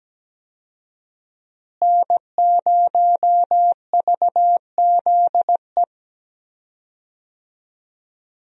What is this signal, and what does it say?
Morse code, heard as a keyed tone, about 17 wpm, 700 Hz. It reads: N0VZE
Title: USA Call Signs - 5 Characters - Repeated 3x - 17wpm